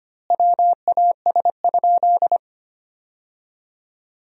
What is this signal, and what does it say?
Morse code, heard as a keyed tone, about 25 wpm, 700 Hz. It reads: WAS?
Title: 4x Word-Spacing - Sentences from Top 200 Words - 25wpm